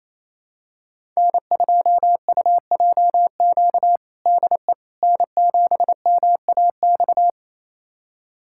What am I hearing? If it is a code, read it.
Morse code, heard as a keyed tone, about 28 wpm, 700 Hz. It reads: N2UJQ DE N7MAX
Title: USA Callsign DE Callsign - 5 Characters - 28wpm